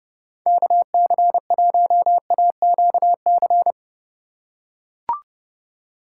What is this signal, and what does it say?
Morse code, heard as a keyed tone, about 30 wpm, 700 Hz. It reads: KC1AQC